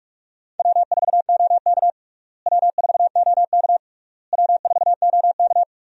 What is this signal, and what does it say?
Morse code, heard as a keyed tone, about 46 wpm, 700 Hz. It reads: W4OK W4OK W4OK
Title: USA Call Signs - 4 Characters - Repeated 3x - 45wpm